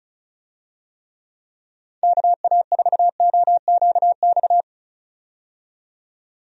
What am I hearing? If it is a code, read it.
Morse code, heard as a keyed tone, about 35 wpm, 700 Hz. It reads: KA4OQX
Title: USA Call Signs - 6 Characters - 35wpm